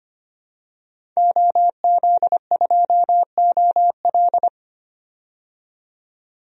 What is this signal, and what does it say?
Morse code, heard as a keyed tone, about 25 wpm, 700 Hz. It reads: OZ2OL
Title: DXCC-100 Call Signs - 5 Characters - 25wpm